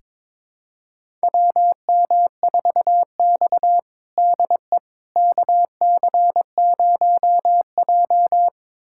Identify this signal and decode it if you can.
Morse code, heard as a keyed tone, about 21 wpm, 700 Hz. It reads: WM4X DE KC0J
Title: USA Callsign DE Callsign - 4 Characters - 22wpm